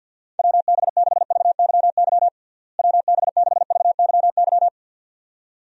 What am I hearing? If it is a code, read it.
Morse code, heard as a keyed tone, about 49 wpm, 700 Hz. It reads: WB6VYY WB6VYY
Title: USA Call Signs - 6 Characters - Repeated 2x - 50wpm